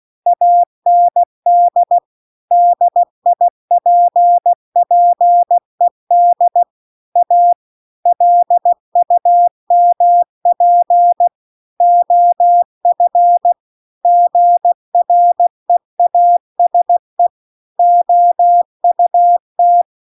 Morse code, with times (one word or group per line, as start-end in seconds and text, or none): 0.3-2.0 AND
2.5-6.6 DIPPED
7.2-7.5 A
8.1-11.3 LUMP
11.8-13.5 OF
14.0-17.3 GREASE
17.8-19.8 OUT